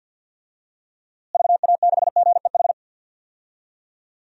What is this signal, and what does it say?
Morse code, heard as a keyed tone, about 53 wpm, 700 Hz. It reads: VA6GEH
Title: DXCC-100 Call Signs - 6 Characters - Repeated 3x - 50wpm